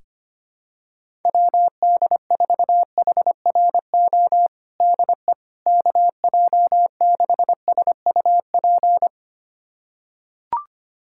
Morse code, heard as a keyed tone, about 25 wpm, 700 Hz. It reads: WD4HRO DE KJ6SUP
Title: USA Callsign DE Callsign 25wpm